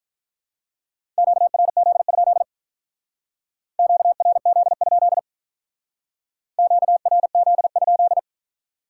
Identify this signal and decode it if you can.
Morse code, heard as a keyed tone, about 46 wpm, 700 Hz. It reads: QRZ? QRZ? QRZ?